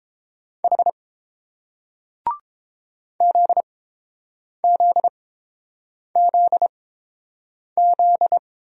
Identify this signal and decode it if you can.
Morse code, heard as a keyed tone, about 31 wpm, spaced as for 7 wpm, 700 Hz. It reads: H ZZZZ